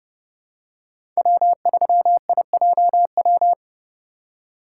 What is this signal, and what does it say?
Morse code, heard as a keyed tone, about 30 wpm, 700 Hz. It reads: W3IJW
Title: USA Call Signs - 5 Characters - 2x Character Spacing - 30wpm